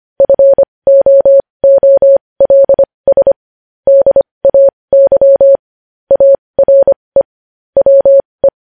Morse code, heard as a keyed tone, about 25 wpm, spaced as for 20 wpm, 550 Hz. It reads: FOOLS DAY ARE WE
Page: news-headlines-in-morse-code-at-20-wpm